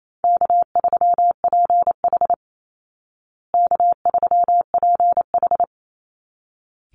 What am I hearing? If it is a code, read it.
Morse code, heard as a keyed tone, about 28 wpm, 700 Hz. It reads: K3PH K3PH
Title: Call Signs Repeated 3x - 28wpm